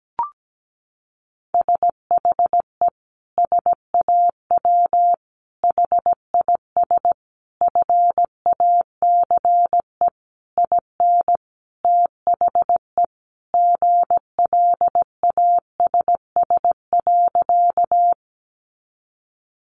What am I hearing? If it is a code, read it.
Morse code, heard as a keyed tone, about 17 wpm, 700 Hz. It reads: SHE SAW HIS FACE IN THE GLASS.
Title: Sentences from Top 1300 Words 17wpm